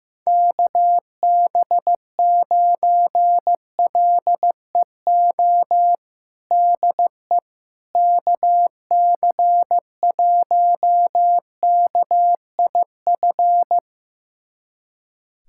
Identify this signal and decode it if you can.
Morse code, heard as a keyed tone, about 15 wpm, 700 Hz. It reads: KB9LEO DE KC1KIF
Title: USA Callsign DE Callsign - 6 Characters - 15wpm